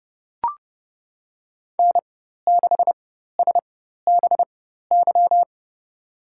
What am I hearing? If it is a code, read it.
Morse code, heard as a keyed tone, about 30 wpm, spaced as for 14 wpm, 700 Hz. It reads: N6SBY
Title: USA Call Signs - 5 Characters - 4x Character Spacing - 30wpm